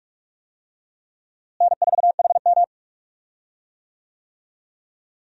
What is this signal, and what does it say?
Morse code, heard as a keyed tone, about 45 wpm, 700 Hz. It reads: N4HM
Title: USA Call Signs - 4 Characters - Repeated 3x - 45wpm